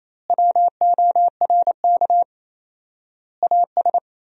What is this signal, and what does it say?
Morse code, heard as a keyed tone, about 27 wpm, 700 Hz. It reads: WORK AS
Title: 4x Word-Spacing - Sentences from Top 500 Words - 28wpm